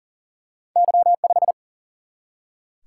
Morse code, heard as a keyed tone, about 41 wpm, 700 Hz. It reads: Y5
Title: Single-Letter Single-Number 40wpm